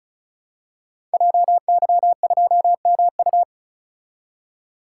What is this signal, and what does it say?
Morse code, heard as a keyed tone, about 34 wpm, 700 Hz. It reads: JY2MU
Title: Code Groups - 5 Letter:Numbers - 35wpm